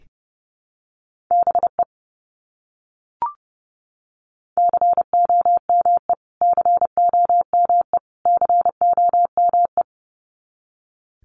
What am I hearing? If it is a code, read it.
Morse code, heard as a keyed tone, about 30 wpm, 700 Hz. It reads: BE COME COME COME
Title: Top 200 Words Repeated 30wpm